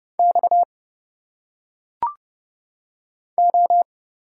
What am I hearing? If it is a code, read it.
Morse code, heard as a keyed tone, about 30 wpm, 700 Hz. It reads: X O
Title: Single Letters - Phonetic Pronunciation 30wpm